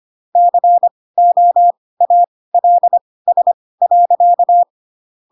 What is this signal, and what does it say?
Morse code, heard as a keyed tone, about 25 wpm, spaced as for 18 wpm, 700 Hz. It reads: COALS.
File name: Pinocchio / Pinocchio ch5 12 25 ik1hge